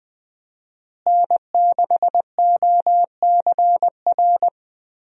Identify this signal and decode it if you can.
Morse code, heard as a keyed tone, about 20 wpm, 700 Hz. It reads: N6OCR
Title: USA Call Signs - 5 Characters - Repeated 2x - 20wpm